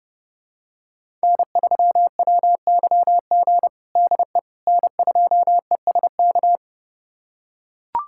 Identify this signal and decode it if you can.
Morse code, heard as a keyed tone, about 30 wpm, 700 Hz. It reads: N3WYG DE N2ESK
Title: USA Callsign DE Callsign - 5 Characters - 30wpm